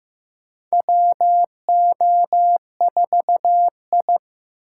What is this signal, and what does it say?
Morse code, heard as a keyed tone, about 16 wpm, 700 Hz. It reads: WO4I
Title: USA Call Signs - 4 Characters - 2x Character Spacing - 15wpm